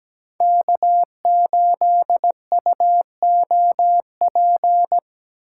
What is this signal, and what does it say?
Morse code, heard as a keyed tone, about 19 wpm, 700 Hz. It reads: K8UOP
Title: Speed Racing - USA 1x3 Call Signs 17wpm